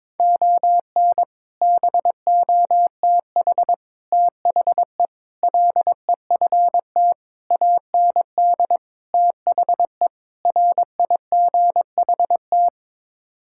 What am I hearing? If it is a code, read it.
Morse code, heard as a keyed tone, about 22 wpm, 700 Hz. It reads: ON BOTH THE LEFT AND THE RIGHT
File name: practice / 22wpm / Sets of 7 Words - 22wpm